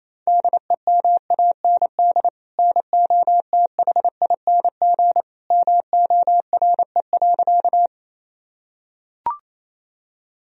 Morse code, heard as a keyed tone, about 28 wpm, 700 Hz. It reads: DEMAND NOTHING MORE.
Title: Sentences from Top 1000 Words - Review Part 4 - 28wpm